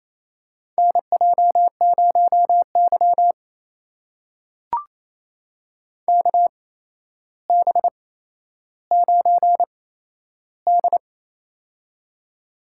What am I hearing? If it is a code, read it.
Morse code, heard as a keyed tone, about 28 wpm, spaced as for 12 wpm, 700 Hz. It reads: NJ0Y KB9D
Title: USA Call Signs - 4 Characters - 8x Character Spacing - 28wpm